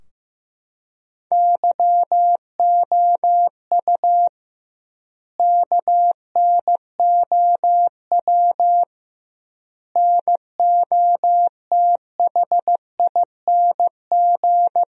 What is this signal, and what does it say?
Morse code, heard as a keyed tone, about 15 wpm, 700 Hz. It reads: YOU KNOW NOTHING
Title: 2x Word-Spacing - Sentences from Top 500 Words - 15wpm